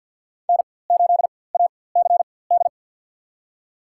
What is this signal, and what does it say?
Morse code, heard as a keyed tone, about 52 wpm, spaced as for 23 wpm, 700 Hz. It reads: N8ACD
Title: USA Call Signs - 5 Characters - 4x Character Spacing - 50wpm